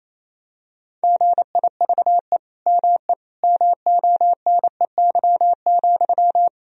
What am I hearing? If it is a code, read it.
Morse code, heard as a keyed tone, about 28 wpm, 700 Hz. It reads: GIVE ME MONEY,
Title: Sentences from Top 1000 Words - Review Part 2 - 28wpm